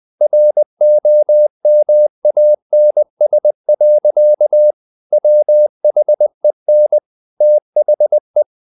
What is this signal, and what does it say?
Morse code, heard as a keyed tone, about 20 wpm, 600 Hz. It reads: ROMANS. WHEN THE
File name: CW / Practice / EnglishSayings20